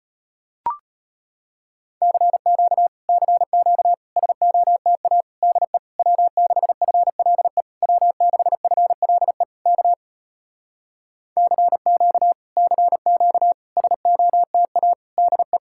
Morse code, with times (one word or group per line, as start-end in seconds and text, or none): 2.0-2.9 CQ
3.1-3.9 CQ
4.2-5.2 SOTA
5.4-5.8 DE
6.0-7.6 W6FLE
7.8-9.4 W6FLE
9.7-9.9 K
11.4-12.3 CQ
12.6-13.5 CQ
13.8-14.9 SOTA
15.2-15.6 DE